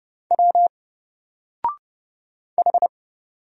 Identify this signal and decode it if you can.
Morse code, heard as a keyed tone, about 29 wpm, 700 Hz. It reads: W H
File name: Single-Letter-Number-30wpm